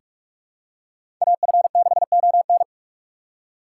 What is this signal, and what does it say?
Morse code, heard as a keyed tone, about 47 wpm, 700 Hz. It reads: AF6ON
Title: USA Call Signs - 5 Characters - 8x Character Spacing - 45wpm